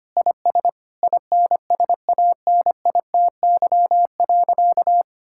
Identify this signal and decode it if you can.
Morse code, heard as a keyed tone, about 25 wpm, 700 Hz. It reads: IS INSANITY.
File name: practice / 25wpm / Sentences with 9-Word Phrases - 25wpm